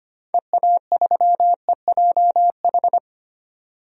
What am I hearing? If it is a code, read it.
Morse code, heard as a keyed tone, about 25 wpm, 700 Hz. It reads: EA3EJH
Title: DXCC-100 Call Signs - 6 Characters - 25wpm